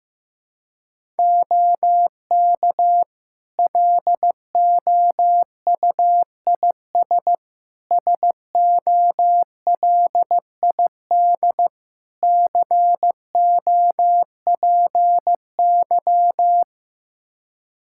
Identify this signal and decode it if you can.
Morse code, heard as a keyed tone, about 15 wpm, 700 Hz. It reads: OK LOUIS SOLID COPY
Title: Practice QSO - Exchange 3 - More Info - 15wpm